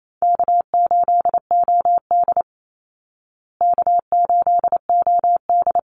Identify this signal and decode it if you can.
Morse code, heard as a keyed tone, about 28 wpm, 700 Hz. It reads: K8OD K8OD
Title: Easy Call Signs Repeated 2x - 28wpm